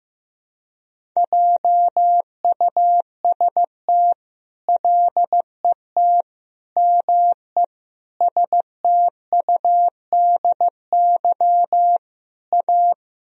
Morse code, heard as a keyed tone, about 15 wpm, 700 Hz. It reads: JUST LET ME STUDY A
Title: Sentences from Top 1000 Words - Review Part 1 - 15wpm